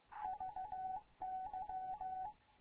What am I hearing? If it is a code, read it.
Morse code, heard as a keyed tone, about 15 wpm, 750 Hz, with radio noise. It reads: VY
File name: Morze-HA2MN / hangp / 2 11 GYAKORLAT